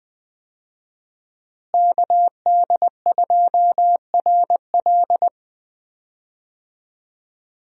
Morse code, heard as a keyed tone, about 20 wpm, 700 Hz. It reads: KD2RL
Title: USA Call Signs - 5 Characters - Repeated 2x - 20wpm